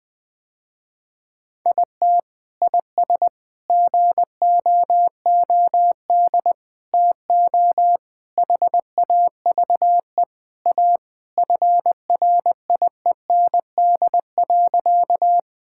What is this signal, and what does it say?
Morse code, heard as a keyed tone, about 21 wpm, 700 Hz. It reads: IT IS GOOD TO HAVE A FRIEND.
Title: Sentences with 3-Word Phrases - Spoken Beginning - 20wpm